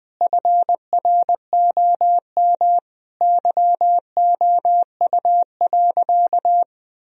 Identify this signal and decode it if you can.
Morse code, heard as a keyed tone, about 20 wpm, 700 Hz. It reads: FROM YOU.
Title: Sentences from Top 1000 Words - Review Part 1 - 20wpm